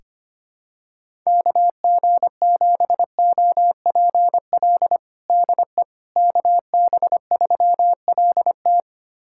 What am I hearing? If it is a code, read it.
Morse code, heard as a keyed tone, about 25 wpm, 700 Hz. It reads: KG7OPL DE KB3LT
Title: USA Callsign DE Callsign 25wpm